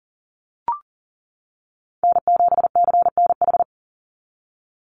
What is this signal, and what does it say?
Morse code, heard as a keyed tone, about 40 wpm, 700 Hz. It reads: N7CNH
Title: NA SOTA Chaser Callsigns 40wpm